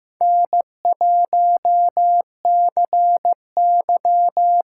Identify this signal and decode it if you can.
Morse code, heard as a keyed tone, about 15 wpm, 700 Hz. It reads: N1CY
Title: USA Call Signs - 5 Characters - 6x Character Spacing - 15wpm